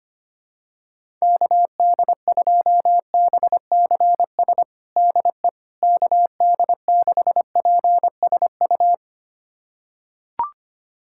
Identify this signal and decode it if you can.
Morse code, heard as a keyed tone, about 25 wpm, 700 Hz. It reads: KD2BCS DE KD6PSU
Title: USA Callsign DE Callsign 25wpm